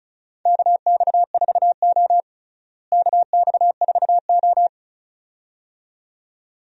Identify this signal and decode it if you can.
Morse code, heard as a keyed tone, about 36 wpm, 700 Hz. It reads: KX4O KX4O
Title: USA Call Signs - 4 Characters - Repeated 2x - 35wpm